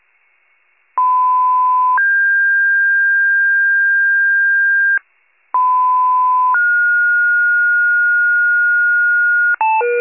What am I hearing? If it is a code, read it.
Two-tone page, heard as a one-second tone, then a three-second tone, about 1010 then 1600 Hz, about 1010 then 1430 Hz, with noise. It is heard from 1.0 to 9.6 s.